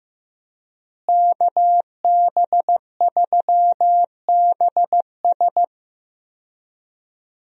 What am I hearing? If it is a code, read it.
Morse code, heard as a keyed tone, about 15 wpm, 700 Hz. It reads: KB3BS
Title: USA Call Signs - 5 Characters - Repeated 3x - 15wpm